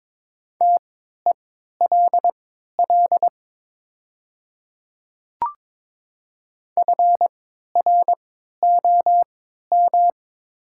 Morse code, heard as a keyed tone, about 23 wpm, spaced as for 13 wpm, 700 Hz. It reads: TELL FROM
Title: 3x Character Spacing - Top 200 Words - 22wpm